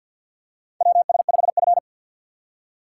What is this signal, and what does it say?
Morse code, heard as a keyed tone, about 53 wpm, 700 Hz. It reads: WS5F
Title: USA Call Signs - 4 Characters - 3x Character Spacing - 50wpm